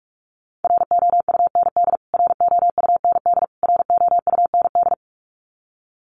Morse code, heard as a keyed tone, about 44 wpm, 700 Hz. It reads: ROUND ROUND ROUND
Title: Top 500 Words Repeated 45wpm